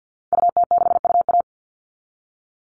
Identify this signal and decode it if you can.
Morse code, heard as a keyed tone, about 50 wpm, 700 Hz. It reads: UT6UA
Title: Call Signs Repeated 2x - 50wpm